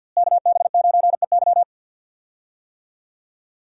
Morse code, heard as a keyed tone, about 50 wpm, 700 Hz. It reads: KB9EY
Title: USA Call Signs - 5 Characters - Repeated 2x - 50wpm